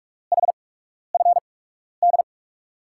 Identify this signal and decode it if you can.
Morse code, heard as a keyed tone, about 41 wpm, spaced as for 12 wpm, 700 Hz. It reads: HFD
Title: USA Call Signs - 6 Characters - 8x Character Spacing - 45wpm